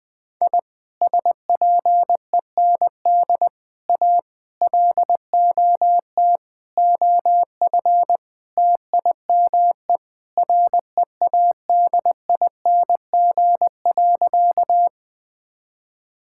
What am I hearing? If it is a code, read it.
Morse code, heard as a keyed tone, about 21 wpm, 700 Hz. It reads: I SPEND A LOT OF TIME READING.
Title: Sentences with 5-Word Phrases - 20wpm